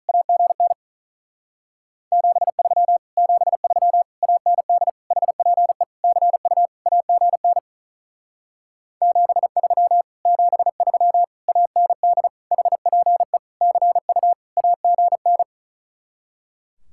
Morse code, heard as a keyed tone, about 46 wpm, 700 Hz. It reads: AGN 73 73 AND HPE CU AGN 73 73 AND HPE CU AGN